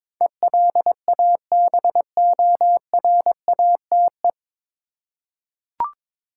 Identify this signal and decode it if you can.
Morse code, heard as a keyed tone, about 22 wpm, 700 Hz. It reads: ELABORATE
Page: Morse Code Ninja practice 22wpm